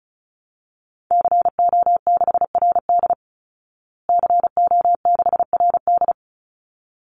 Morse code, heard as a keyed tone, about 34 wpm, 700 Hz. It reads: CO6RD CO6RD
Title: Call Signs Repeated 3x - 35wpm